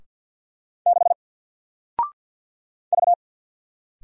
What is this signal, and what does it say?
Morse code, heard as a keyed tone, about 50 wpm, 700 Hz. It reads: 6 V
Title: Single Letter-Number 50wpm